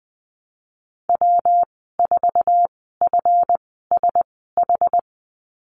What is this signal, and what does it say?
Morse code, heard as a keyed tone, about 20 wpm, spaced as for 14 wpm, 700 Hz. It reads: W4FSH
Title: 140 Lesson 23g - Letter f - Call Signs 20wpm